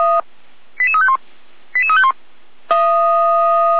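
Five-tone selective call (ZVEI), heard as quick runs of five tones, tones of about 70 ms. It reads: 88351 88351
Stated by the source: Five-tone ZVEI selective call received by SDR.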